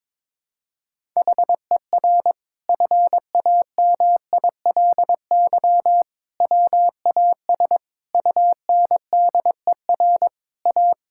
Morse code, heard as a keyed tone, about 22 wpm, 700 Hz. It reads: HER FAMILY WAS UNDER A
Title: Sentences from Top 1000 Words - Review Part 6 - 22wpm